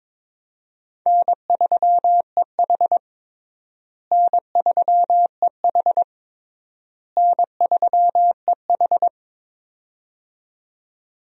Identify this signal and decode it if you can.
Morse code, heard as a keyed tone, about 22 wpm, 700 Hz. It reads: N3EH N3EH N3EH
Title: USA Call Signs - 4 Characters - Repeated 3x - 22wpm